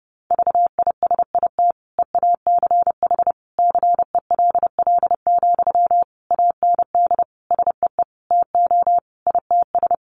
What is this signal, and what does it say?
Morse code, heard as a keyed tone, about 30 wpm, 700 Hz. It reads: VISIT EACH CELL, AND SEE TO ITS
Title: Sentences from Top 1100 Words 30wpm